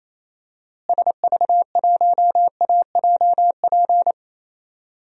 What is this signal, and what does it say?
Morse code, heard as a keyed tone, about 28 wpm, 700 Hz. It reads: SV1AJP
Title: DXCC-100 Call Signs - 6 Characters - Repeated 3x - 28wpm